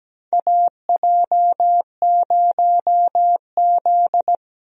Morse code, heard as a keyed tone, about 17 wpm, 700 Hz. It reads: AJ0Z
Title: USA Call Signs - 4 Characters - 17wpm